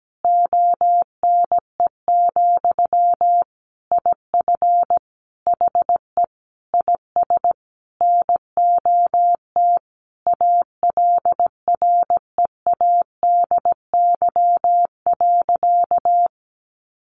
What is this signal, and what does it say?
Morse code, heard as a keyed tone, about 17 wpm, 700 Hz. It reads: ONE, IF HE IS NOT ALREADY.
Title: Sentences from Top 900 Words 17wpm